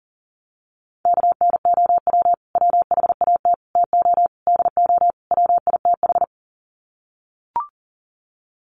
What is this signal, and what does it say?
Morse code, heard as a keyed tone, about 40 wpm, 700 Hz. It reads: KNOW WHAT TO DO WITH
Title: Sets of 5 Words 40wpm